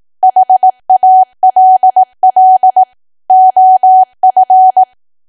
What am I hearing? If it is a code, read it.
Morse code, heard as a keyed tone, about 18 wpm, 750 Hz. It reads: HALL OF
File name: Archive / 18 WPM / 18WPM